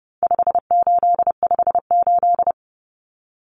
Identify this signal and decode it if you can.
Morse code, heard as a keyed tone, about 30 wpm, 700 Hz. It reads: 5858
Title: Numbers - 4-Digit 30wpm